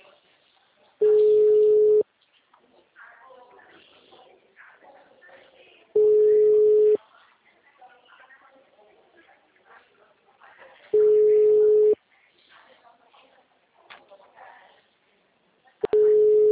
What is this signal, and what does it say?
Ringback tone (Europe, 425 Hz), with speech.
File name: out- -2016- - - .38748